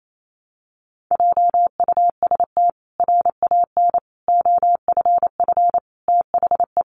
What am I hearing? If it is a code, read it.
Morse code, heard as a keyed tone, about 28 wpm, 700 Hz. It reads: JUST RAN OFF THE
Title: A Sound Of Thunder 28wpm